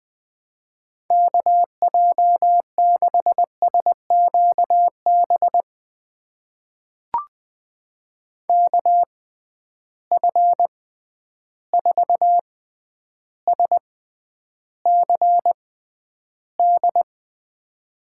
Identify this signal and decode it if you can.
Morse code, heard as a keyed tone, about 20 wpm, spaced as for 10 wpm, 700 Hz. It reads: KJ6SQB KF4SCD
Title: USA Call Signs - 6 Characters - 6x Character Spacing - 20wpm